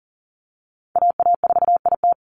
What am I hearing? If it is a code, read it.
Morse code, heard as a keyed tone, about 40 wpm, 700 Hz. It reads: AA4IT
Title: NA SOTA Chaser Callsigns 40wpm